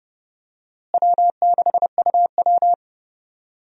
Morse code, heard as a keyed tone, about 32 wpm, 700 Hz. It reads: W6UW
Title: USA Call Signs - 4 Characters - Repeated 3x - 30wpm